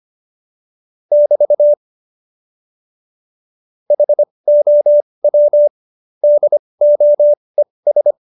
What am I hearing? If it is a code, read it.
Morse code, heard as a keyed tone, about 25 wpm, spaced as for 20 wpm, 590 Hz. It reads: <BT> HOW DOES
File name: Jokes-25-char-20-eff